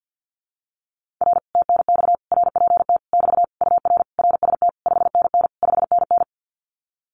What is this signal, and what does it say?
Morse code, heard as a keyed tone, about 50 wpm, 700 Hz. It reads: R TNX RPT <BT> UR RST 5NN 5NN
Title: Common QSO Phrases Encore 50wpm